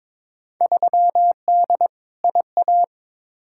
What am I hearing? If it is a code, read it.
Morse code, heard as a keyed tone, about 22 wpm, 700 Hz. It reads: 3D IA
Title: Field Day Exchanges Encore - 22wpm